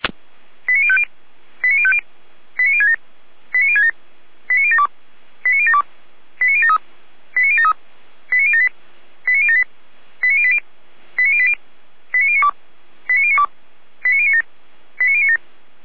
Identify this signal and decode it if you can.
Five-tone selective call (ZVEI), heard as quick runs of five tones, tones of about 70 ms. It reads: 89050 89050 89067 89067 89072 89072 89073 89073 89078 89078 89080 89080 89092 89092 89097 89097